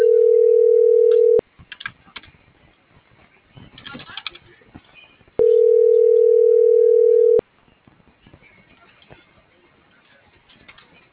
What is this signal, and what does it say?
Ringback tone (North America), 440 + 480 Hz.